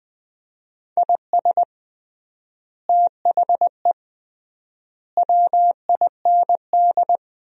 Morse code, heard as a keyed tone, about 20 wpm, 700 Hz. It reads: IS THE WIND
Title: 3x Word-Spacing - Sentences from Top 400 Words - 20wpm